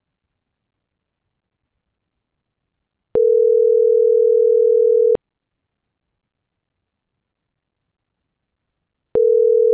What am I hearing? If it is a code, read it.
Ringback tone (North America), 440 + 480 Hz.